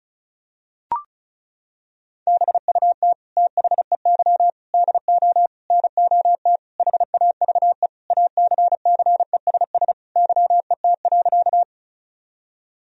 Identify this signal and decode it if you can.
Morse code, heard as a keyed tone, about 34 wpm, 700 Hz. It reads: BUT THEY DO NOT HAVE ACCESS YET.
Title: Sentences from Top 1000 Words - Review Part 4 - 35wpm